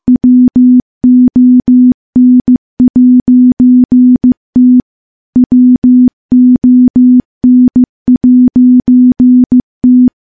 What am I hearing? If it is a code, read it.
Morse code, heard as a keyed tone, about 15 wpm, 250 Hz. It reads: WON'T WON'T
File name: morse-training-15WPM